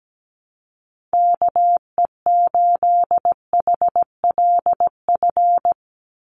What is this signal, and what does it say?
Morse code, heard as a keyed tone, about 17 wpm, 700 Hz. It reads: KE8HLF
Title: USA Call Signs 17wpm